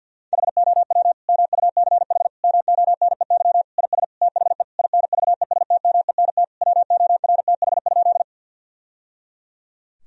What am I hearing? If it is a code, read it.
Morse code, heard as a keyed tone, about 50 wpm, 700 Hz. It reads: HOW MUCH MONEY IS THE INVESTMENT WORTH?
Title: Sentences from Top 1000 Words - Review Part 4 - 50wpm